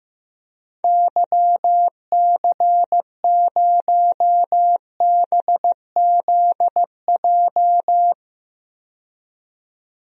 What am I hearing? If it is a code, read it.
Morse code, heard as a keyed tone, about 15 wpm, 700 Hz. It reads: YC0BZJ
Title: DXCC-100 Call Signs - 6 Characters - 15wpm